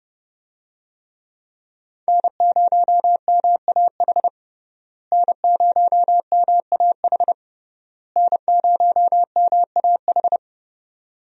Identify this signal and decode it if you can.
Morse code, heard as a keyed tone, about 30 wpm, 700 Hz. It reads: N0MAH N0MAH N0MAH
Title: USA Call Signs - 5 Characters - Repeated 3x - 30wpm